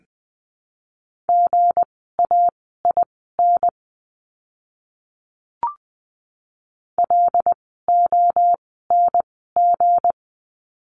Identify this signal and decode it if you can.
Morse code, heard as a keyed tone, about 20 wpm, spaced as for 15 wpm, 700 Hz. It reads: GAIN LONG
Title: 160 Lesson 26f - Letter g - Words 20wpm